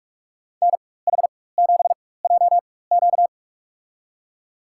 Morse code, heard as a keyed tone, about 46 wpm, spaced as for 21 wpm, 700 Hz. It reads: NH7JQ
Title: USA Call Signs - 5 Characters - 4x Character Spacing - 45wpm